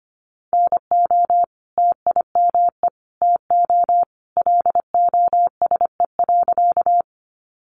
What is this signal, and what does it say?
Morse code, heard as a keyed tone, about 25 wpm, 700 Hz. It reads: NO TIME TO LOSE.
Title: Sentences from Top 1100 Words 25wpm